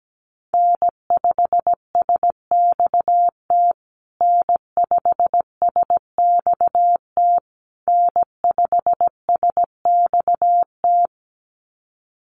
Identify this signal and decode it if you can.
Morse code, heard as a keyed tone, about 17 wpm, 700 Hz. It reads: N5SXT N5SXT N5SXT
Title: USA Call Signs Repeated 3x 17wpm